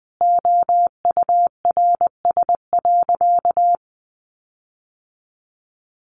Morse code, heard as a keyed tone, about 20 wpm, 700 Hz. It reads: OURS.
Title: Sentences from Top 1300 Words 20wpm